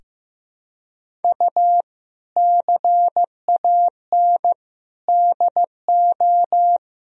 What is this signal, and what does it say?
Morse code, heard as a keyed tone, about 15 wpm, 700 Hz. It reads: U CAN DO
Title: Sentences from Top 100 Words and Abbreviations 15wpm